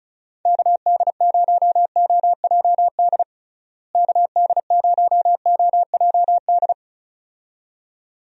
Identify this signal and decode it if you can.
Morse code, heard as a keyed tone, about 34 wpm, 700 Hz. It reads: KD0OJD KD0OJD
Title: USA Call Signs - 6 Characters - Repeated 2x - 35wpm